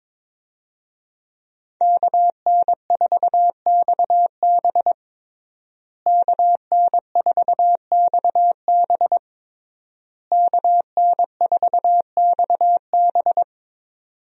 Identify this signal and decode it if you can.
Morse code, heard as a keyed tone, about 22 wpm, 700 Hz. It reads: KN4XB KN4XB KN4XB
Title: USA Call Signs - 5 Characters - Repeated 3x - 22wpm